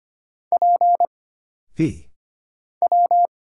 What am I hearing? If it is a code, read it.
Morse code, heard as a keyed tone, about 25 wpm, 700 Hz. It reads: P W
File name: Single Character (Letter:Number:Punctuation:Prosign) - Mind-Melt - 25wpm